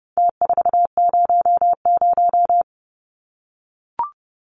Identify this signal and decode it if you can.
Morse code, heard as a keyed tone, about 30 wpm, 700 Hz. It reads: T400
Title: Numbers - Contest Style - 4-Digit 30wpm